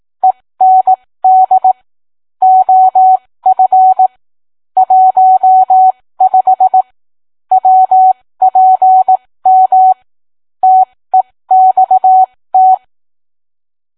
Morse code, heard as a keyed tone, about 18 wpm, spaced as for 15 wpm, 750 Hz. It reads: END OF 15 WPM TEXT